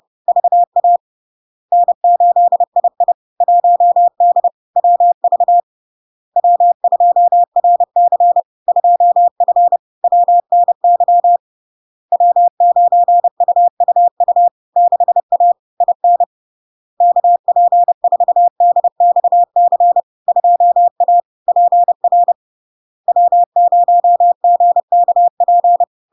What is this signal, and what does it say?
Morse code, heard as a keyed tone, about 30 wpm, 700 Hz. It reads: VA N8II 1D WV W2RC 2F WNY W9UUU 6A IN KP4DXC 2A PR W0GKP
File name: LARK-FD-2022-Log-in-CW-30wpm-700-Hz